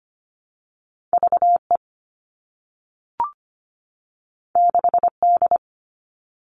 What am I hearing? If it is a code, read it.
Morse code, heard as a keyed tone, about 25 wpm, 700 Hz. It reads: VE 6D
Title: All Cut Numbers - 2-Digit 25wpm